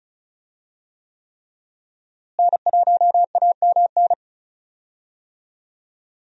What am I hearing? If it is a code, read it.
Morse code, heard as a keyed tone, about 34 wpm, 700 Hz. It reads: N1AMN
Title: USA Call Signs - 5 Characters - Repeated 3x - 35wpm